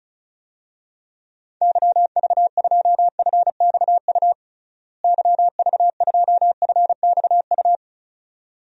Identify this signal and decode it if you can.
Morse code, heard as a keyed tone, about 34 wpm, 700 Hz. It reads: YV2FXU YV2FXU
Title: DXCC-100 Call Signs - 6 Characters - Repeated 2x - 35wpm